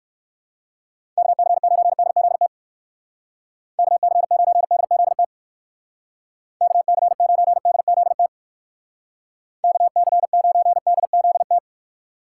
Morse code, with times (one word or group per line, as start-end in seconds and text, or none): 1.2-2.5 KC9DZT
3.8-5.2 KC9DZT
6.6-8.3 KC9DZT
9.6-11.6 KC9DZT